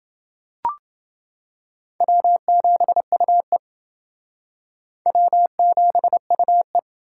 Morse code, heard as a keyed tone, about 29 wpm, 700 Hz. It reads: W7UE W7UE